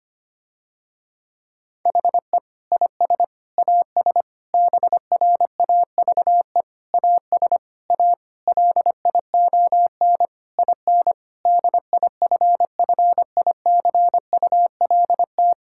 Morse code, heard as a keyed tone, about 25 wpm, 700 Hz. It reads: HE IS AS BRAVE AS A LION IN DIFFICULT
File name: practice / 25wpm / Difficult Sentences with Hints - 25wpm